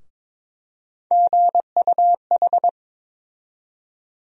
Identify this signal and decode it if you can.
Morse code, heard as a keyed tone, about 21 wpm, 700 Hz. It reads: GUH
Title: USA Call Sign Postfixes - 3 Letters 22wpm